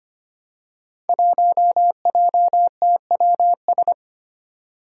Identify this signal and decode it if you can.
Morse code, heard as a keyed tone, about 25 wpm, 700 Hz. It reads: 1JTWS
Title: Five Challenging Letter-Number Combinations 25wpm